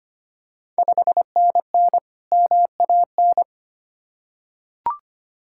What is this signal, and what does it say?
Morse code, heard as a keyed tone, about 25 wpm, 700 Hz. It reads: 5NN MAN
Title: Ontario QSO Party Exchange 25wpm